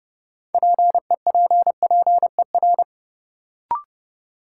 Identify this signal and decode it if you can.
Morse code, heard as a keyed tone, about 32 wpm, 700 Hz. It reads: PEPPER
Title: Sets of 5 Words - New Word Drill - 30wpm